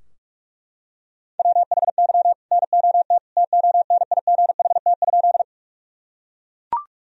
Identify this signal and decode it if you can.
Morse code, heard as a keyed tone, about 41 wpm, 700 Hz. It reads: WHY NOT TONIGHT?
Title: Sentences from Top 1000 Words - Review Part 3 - 45wpm